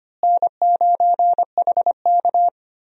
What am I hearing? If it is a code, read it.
Morse code, heard as a keyed tone, about 25 wpm, 700 Hz. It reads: N9HK
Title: USA Call Signs - 4 Characters - Repeated 3x - 25wpm